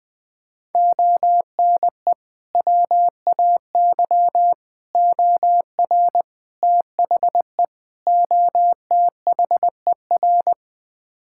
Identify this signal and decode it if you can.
Morse code, heard as a keyed tone, about 20 wpm, 700 Hz. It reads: ONE WAY OR THE OTHER
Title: Build A Phrase - 5-Word Phrases - 20wpm